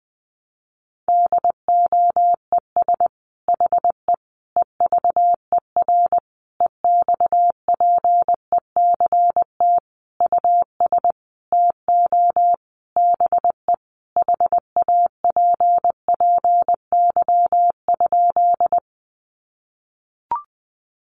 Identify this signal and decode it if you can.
Morse code, heard as a keyed tone, about 20 wpm, 700 Hz. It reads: DOES HE EVER EXPECT US TO BE HAPPY?
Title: Sentences from Top 1100 Words 20wpm